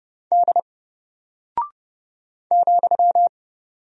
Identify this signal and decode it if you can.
Morse code, heard as a keyed tone, about 30 wpm, 700 Hz. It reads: D ,
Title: Single Character (Letter:Number:Punctuation:Prosign) Alt 30wpm